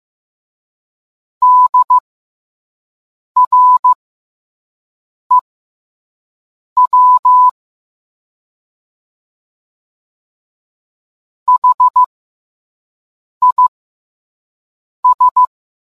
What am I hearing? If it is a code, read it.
Morse code, heard as a keyed tone, about 15 wpm, spaced as for 5 wpm, 1000 Hz. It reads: DREW HIS